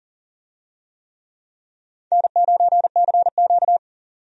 Morse code, heard as a keyed tone, about 40 wpm, 700 Hz. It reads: N9CQ
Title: USA Call Signs - 4 Characters - Repeated 2x - 40wpm